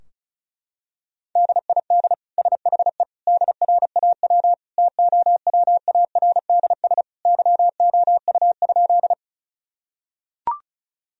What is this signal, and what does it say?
Morse code, heard as a keyed tone, about 36 wpm, 700 Hz. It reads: DID SHE DRAW TOWARDS YOU?
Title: Sentences from Top 1000 Words - Review Part 2 - 35wpm